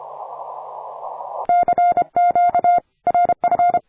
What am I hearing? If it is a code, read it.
Morse code, heard as a keyed tone, about 25 wpm, 700 Hz. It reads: CQ RF
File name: attachments / 40m UU7JF P